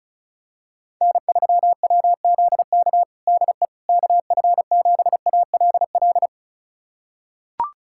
Morse code, heard as a keyed tone, about 36 wpm, 700 Hz. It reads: N3WZK DE KF7ALL
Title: USA Callsign DE Callsign 35wpm